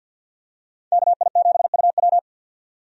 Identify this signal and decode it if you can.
Morse code, heard as a keyed tone, about 51 wpm, 700 Hz. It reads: KI7UW
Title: USA Call Signs - 5 Characters - 50wpm